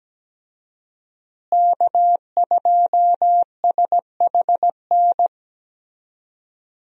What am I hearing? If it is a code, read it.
Morse code, heard as a keyed tone, about 17 wpm, 700 Hz. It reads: K2SHN
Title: USA Call Signs - 5 Characters - Repeated 2x - 17wpm